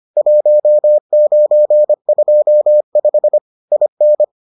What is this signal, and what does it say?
Morse code, heard as a keyed tone, about 25 wpm, 600 Hz. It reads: 1925 IN